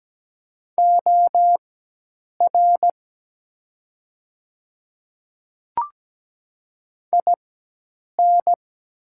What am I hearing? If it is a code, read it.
Morse code, heard as a keyed tone, about 17 wpm, spaced as for 8 wpm, 700 Hz. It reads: OR IN
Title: 4x Character Spacing - Top 100 Words - 17wpm